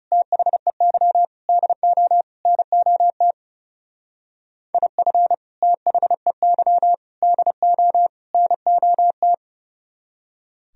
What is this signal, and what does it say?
Morse code, heard as a keyed tone, about 35 wpm, 700 Hz. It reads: THEY DO NOT IF THEY DO NOT
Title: Speed-Racing - Sets of 4 Words 30wpm